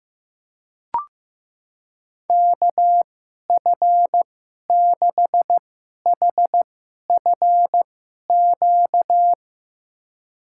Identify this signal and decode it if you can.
Morse code, heard as a keyed tone, about 15 wpm, spaced as for 11 wpm, 700 Hz. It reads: KF6HFQ
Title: USA Call Signs - 6 Characters - 2x Character Spacing - 15wpm